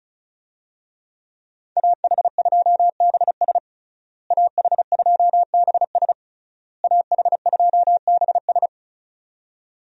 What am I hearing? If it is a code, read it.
Morse code, heard as a keyed tone, about 34 wpm, 700 Hz. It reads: AH2BS AH2BS AH2BS
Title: USA Call Signs - 5 Characters - Repeated 3x - 35wpm